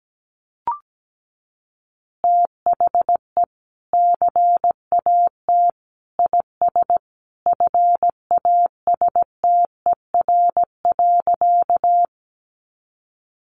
Morse code, heard as a keyed tone, about 17 wpm, 700 Hz. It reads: THE CAT IS FASTER.
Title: Additional Easy Phrases 17wpm